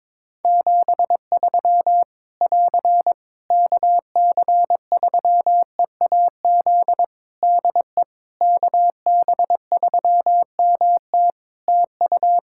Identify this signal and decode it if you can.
Morse code, heard as a keyed tone, about 21 wpm, 700 Hz. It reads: 73 <AR> KC3EAZ DE KB3MT TU
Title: Practice QSO - Exchange 4 - Ending - 22wpm